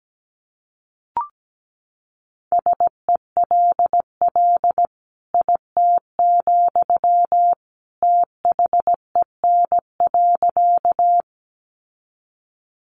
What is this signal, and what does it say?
Morse code, heard as a keyed tone, about 17 wpm, 700 Hz. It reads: SELL IT, THEN.
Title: Sentences from Top 1100 Words 17wpm